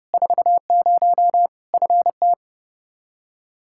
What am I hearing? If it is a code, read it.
Morse code, heard as a keyed tone, about 30 wpm, 700 Hz. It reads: 40 FT
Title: Practice QSO - Exchange 2 - Info - 30wpm